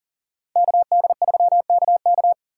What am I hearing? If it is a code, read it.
Morse code, heard as a keyed tone, about 52 wpm, 700 Hz. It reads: KD3KK